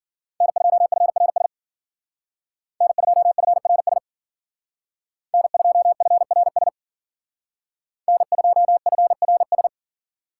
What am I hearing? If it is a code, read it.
Morse code, heard as a keyed tone, about 48 wpm, 700 Hz. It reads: N2FRS N2FRS N2FRS N2FRS
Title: Speed Racing - USA 1x3 Call Signs 40wpm